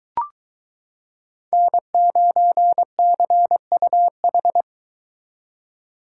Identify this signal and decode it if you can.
Morse code, heard as a keyed tone, about 22 wpm, 700 Hz. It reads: N9CUH